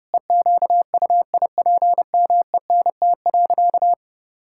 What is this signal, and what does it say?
Morse code, heard as a keyed tone, about 32 wpm, 700 Hz. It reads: EQUIPMENT.
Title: Sentences from Top 1000 Words - Review Part 6 - 30wpm